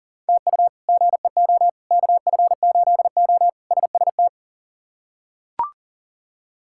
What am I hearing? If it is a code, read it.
Morse code, heard as a keyed tone, about 40 wpm, 700 Hz. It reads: TU GEO KF8O SST
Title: K1USN SST QSO Practice - 40wpm